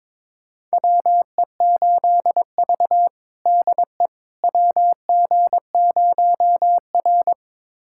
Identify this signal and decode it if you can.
Morse code, heard as a keyed tone, about 22 wpm, 700 Hz. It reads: WE8V DE WG0R
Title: USA Callsign DE Callsign - 4 Characters - 22wpm